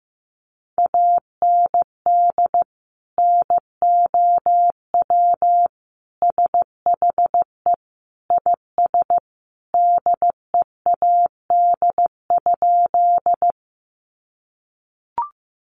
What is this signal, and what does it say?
Morse code, heard as a keyed tone, about 15 wpm, 700 Hz. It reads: AND NOW SHE IS DEAD?
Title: Sentences from Top 1500 Words 15wpm